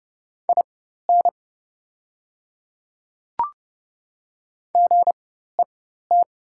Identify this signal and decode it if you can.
Morse code, heard as a keyed tone, about 30 wpm, spaced as for 14 wpm, 700 Hz. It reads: IN GET
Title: 4x Character Spacing - Top 200 Words - 30wpm